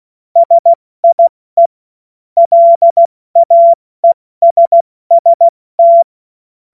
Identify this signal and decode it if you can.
Morse code, heard as a keyed tone, about 16 wpm, 680 Hz. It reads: SIE LAESST
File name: Morsekurs / Witze / Witz 31. August 2021, 70 F80